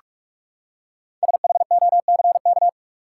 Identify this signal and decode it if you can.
Morse code, heard as a keyed tone, about 44 wpm, 700 Hz. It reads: SHOCK
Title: Five Letter Words - 45wpm